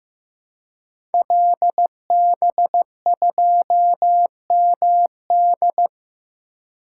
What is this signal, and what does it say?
Morse code, heard as a keyed tone, about 15 wpm, 700 Hz. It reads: LB2MD
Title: Code Groups - 5 Letter:Numbers - 15wpm